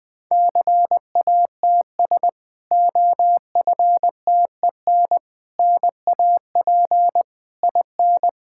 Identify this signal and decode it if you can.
Morse code, heard as a keyed tone, about 21 wpm, 700 Hz. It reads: CATS OFTEN NAP IN
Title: Difficult Sentences with Hints - 20wpm